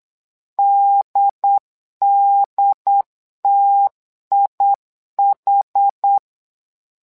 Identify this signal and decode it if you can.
Morse code, heard as a keyed tone, about 8 wpm, 800 Hz. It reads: DDTIH
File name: LEZIONI / 2 UHDN / 2 UHD 8wpm 2258